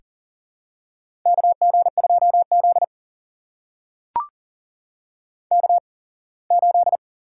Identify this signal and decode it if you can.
Morse code, heard as a keyed tone, about 42 wpm, spaced as for 24 wpm, 700 Hz. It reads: KG2Z K8
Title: USA Call Signs - 4 Characters - 8x Character Spacing - 40wpm